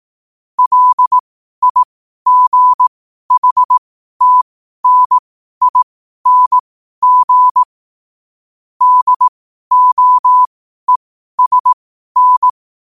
Morse code, heard as a keyed tone, about 18 wpm, spaced as for 12 wpm, 1000 Hz. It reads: LIGHTNING DOESN
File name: qotdfiles / 12wpm